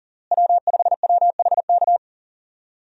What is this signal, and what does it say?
Morse code, heard as a keyed tone, about 40 wpm, 700 Hz. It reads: W5WHK
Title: USA Call Signs - 5 Characters - Repeated 3x - 40wpm